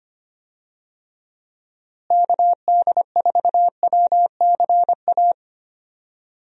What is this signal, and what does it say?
Morse code, heard as a keyed tone, about 25 wpm, 700 Hz. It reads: KD4WCA
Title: USA Call Signs - 6 Characters - Repeated 2x - 25wpm